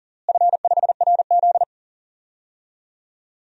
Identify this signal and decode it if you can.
Morse code, heard as a keyed tone, about 39 wpm, 700 Hz. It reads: F5RZ
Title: DXCC-100 Call Signs - 4 Characters - Repeated 3x - 40wpm